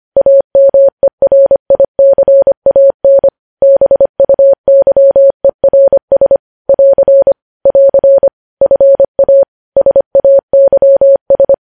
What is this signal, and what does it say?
Morse code, heard as a keyed tone, about 25 wpm, 550 Hz. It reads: AMERICAN BUYERS <AR> <AR> FA SAYS